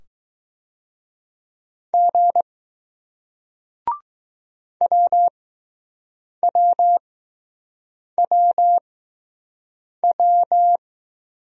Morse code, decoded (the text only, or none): G WWWW